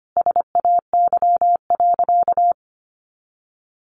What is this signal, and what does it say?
Morse code, heard as a keyed tone, about 25 wpm, 700 Hz. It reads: SAY.
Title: Sentences from Top 500 Words 25wpm